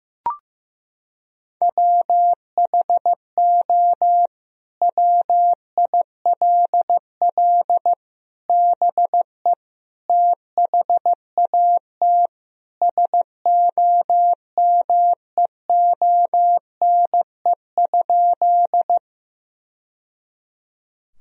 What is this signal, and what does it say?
Morse code, heard as a keyed tone, about 16 wpm, 700 Hz. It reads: WHO WILL BE THAT SOMEONE?
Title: Sentences from Top 1000 Words - Review Part 4 - 15wpm